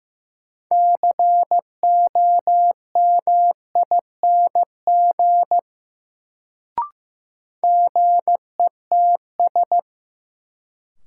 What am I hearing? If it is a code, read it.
Morse code, heard as a keyed tone, about 15 wpm, 700 Hz. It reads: COMING GETS
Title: Sentences from Top 1300 Words - New Word Drill - 15wpm